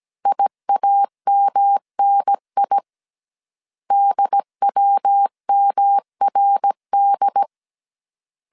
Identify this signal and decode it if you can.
Morse code, heard as a keyed tone, about 17 wpm, 800 Hz. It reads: IAMNI DWMRD